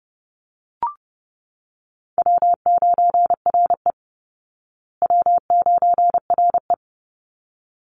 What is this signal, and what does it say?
Morse code, heard as a keyed tone, about 30 wpm, 700 Hz. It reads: W9RE W9RE
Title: Easy Call Signs Repeated 2x - 30wpm